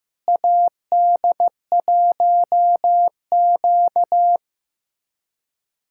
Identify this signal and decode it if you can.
Morse code, heard as a keyed tone, about 15 wpm, 700 Hz. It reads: AD1Q
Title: USA Callsign DE Callsign - 4 Characters - 15wpm